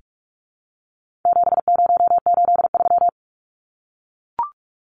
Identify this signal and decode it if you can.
Morse code, heard as a keyed tone, about 44 wpm, 700 Hz. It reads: 7083
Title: Numbers - 4-Digit 45wpm